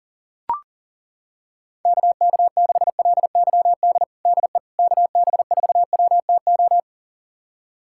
Morse code, heard as a keyed tone, about 40 wpm, 690 Hz. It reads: KK6LYD DE KB4WTO
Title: USA Callsign DE Callsign - 6 Characters - 40wpm